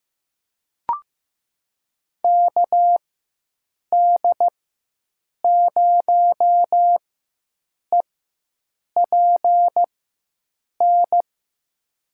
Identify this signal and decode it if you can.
Morse code, heard as a keyed tone, about 16 wpm, spaced as for 7 wpm, 700 Hz. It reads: KD0EPN
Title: USA Call Signs - 6 Characters - 4x Character Spacing - 15wpm